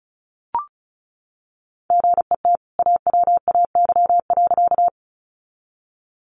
Morse code, heard as a keyed tone, about 34 wpm, 700 Hz. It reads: GET AWAY.
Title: Sentences from Top 600 Words 35wpm